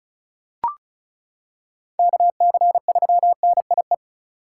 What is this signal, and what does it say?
Morse code, heard as a keyed tone, about 36 wpm, 700 Hz. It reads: KC3NIE
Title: K1USN SST QSO Practice - 35wpm